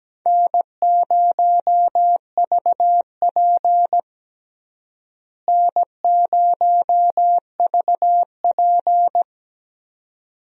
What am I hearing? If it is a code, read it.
Morse code, heard as a keyed tone, about 17 wpm, 700 Hz. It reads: N0VP N0VP
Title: USA Call Signs - 4 Characters - Repeated 2x - 17wpm